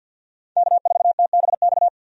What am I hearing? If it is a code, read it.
Morse code, heard as a keyed tone, about 47 wpm, 700 Hz. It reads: K4TBX
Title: USA Call Signs - 5 Characters - 50wpm